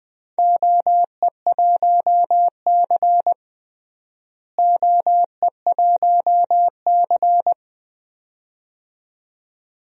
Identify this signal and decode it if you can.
Morse code, heard as a keyed tone, about 20 wpm, 700 Hz. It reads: OE1C OE1C
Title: DXCC-100 Call Signs - 4 Characters - Repeated 2x - 20wpm